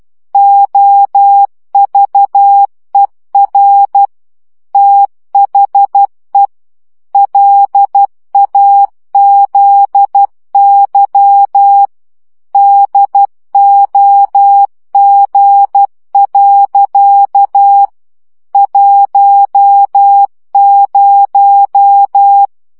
Morse code, heard as a keyed tone, about 12 wpm, 800 Hz. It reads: OVER THE LAZY DOG. 10
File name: CW 10 baud